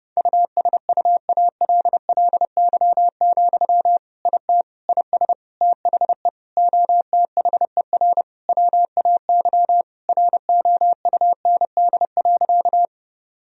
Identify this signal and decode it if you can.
Morse code, heard as a keyed tone, about 30 wpm, 700 Hz. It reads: USUALLY, IT IS THE OTHER WAY ROUND.
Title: Sentences from Top 1000 Words - Review Part 3 - 30wpm